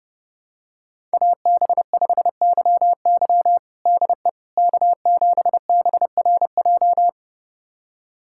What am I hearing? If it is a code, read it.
Morse code, heard as a keyed tone, about 30 wpm, 700 Hz. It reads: AB5YY DE K7BRJ
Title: USA Callsign DE Callsign - 5 Characters - 30wpm